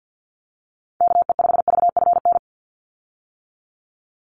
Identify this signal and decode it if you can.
Morse code, heard as a keyed tone, about 50 wpm, 700 Hz. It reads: KE5VFN
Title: USA Call Signs 50wpm